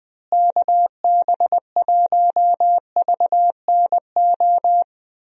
Morse code, heard as a keyed tone, about 20 wpm, 700 Hz. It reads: KB1VNO
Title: USA Call Signs - 6 Characters - Repeated 3x - 20wpm